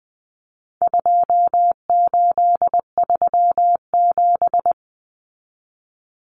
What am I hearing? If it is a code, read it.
Morse code, heard as a keyed tone, about 20 wpm, 700 Hz. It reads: 2837
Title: Common Cut Numbers - 4-Digit 20wpm